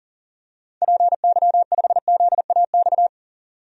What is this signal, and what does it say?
Morse code, heard as a keyed tone, about 40 wpm, 700 Hz. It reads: PY5ZAX
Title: DXCC-100 Call Signs - 6 Characters - 40wpm